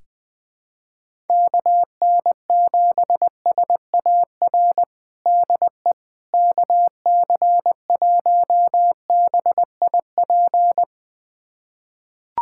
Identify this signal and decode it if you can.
Morse code, heard as a keyed tone, about 21 wpm, 700 Hz. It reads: KN7SAR DE KC1BIP
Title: USA Callsign DE Callsign 20wpm